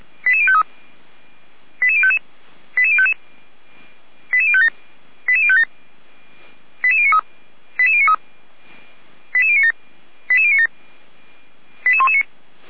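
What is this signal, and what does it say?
Five-tone selective call (ZVEI), heard as quick runs of five tones, tones of about 70 ms. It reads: 88053 88055 88055 88057 88057 88093 88093 88097 88097 88119
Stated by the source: ZVEI five-tone selective call received on SDR, Feuerwehr Orlamünde audio archive.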